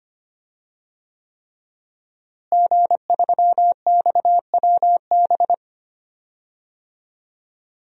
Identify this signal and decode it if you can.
Morse code, heard as a keyed tone, about 25 wpm, 700 Hz. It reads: G3XWB
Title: DXCC-100 Call Signs - 5 Characters - Repeated 3x - 25wpm